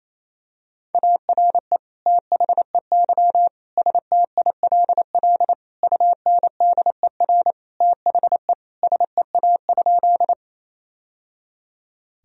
Morse code, heard as a keyed tone, about 28 wpm, 700 Hz. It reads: ARE THEY STILL UNDER THE SEA?
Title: Sentences from Top 1000 Words - Review Part 1 - 28wpm